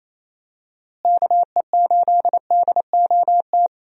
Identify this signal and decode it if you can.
Morse code, heard as a keyed tone, about 28 wpm, 700 Hz. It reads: KE8DOT
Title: USA Call Signs - 6 Characters - Repeated 3x - 28wpm